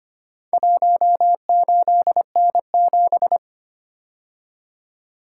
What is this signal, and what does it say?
Morse code, heard as a keyed tone, about 25 wpm, 700 Hz. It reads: 18N7
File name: practice / 25wpm / Four Letter-Number Combinations - 25wpm